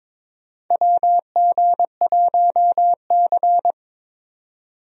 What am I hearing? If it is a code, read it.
Morse code, heard as a keyed tone, about 22 wpm, 700 Hz. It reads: WG1C
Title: USA Call Signs - 4 Characters - 22wpm